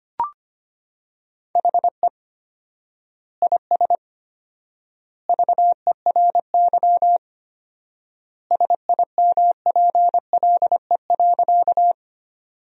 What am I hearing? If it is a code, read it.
Morse code, heard as a keyed tone, about 25 wpm, 700 Hz. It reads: HE IS VERY SIMPLE.
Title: 4x Word-Spacing - Sentences from Top 400 Words - 25wpm